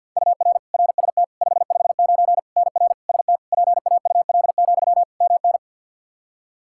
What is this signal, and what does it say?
Morse code, heard as a keyed tone, about 53 wpm, 700 Hz. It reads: UR RST 559 NR ST PAUL, MN
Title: Practice QSO - Exchange 1 - Beginning - 50wpm